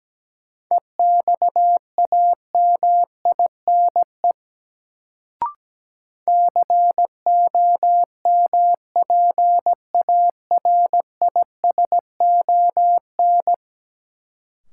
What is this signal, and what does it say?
Morse code, heard as a keyed tone, about 17 wpm, 700 Hz. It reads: EXAMINE COMPARISON
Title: Sentences from Top 1600 Words - New Word Drill Alt - 17wpm